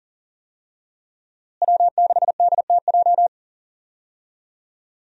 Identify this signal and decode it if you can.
Morse code, heard as a keyed tone, about 40 wpm, 700 Hz. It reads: W6DTJ
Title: USA Call Signs - 5 Characters - Repeated 2x - 40wpm